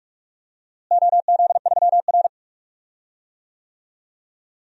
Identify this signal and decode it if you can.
Morse code, heard as a keyed tone, about 46 wpm, 700 Hz. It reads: OZ3R
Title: DXCC-100 Call Signs - 4 Characters - Repeated 2x - 45wpm